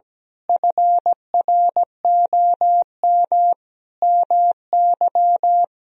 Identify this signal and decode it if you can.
Morse code, heard as a keyed tone, about 17 wpm, 700 Hz. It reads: FROM MY
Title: Sentences from Top 1000 Words - Review Part 1 - 17wpm